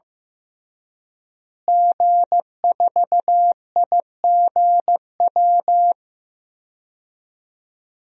Morse code, heard as a keyed tone, about 16 wpm, 700 Hz. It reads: G4IGW
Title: DXCC-100 Call Signs - 5 Characters - Repeated 2x - 15wpm